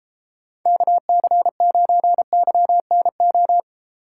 Morse code, heard as a keyed tone, about 32 wpm, 700 Hz. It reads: KC9YNO